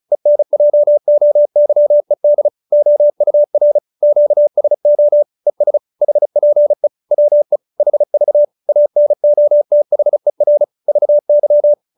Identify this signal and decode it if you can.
Morse code, heard as a keyed tone, about 34 wpm, 590 Hz. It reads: ENJOYED OUR QSO ES HPE WE HV ANOTHER VY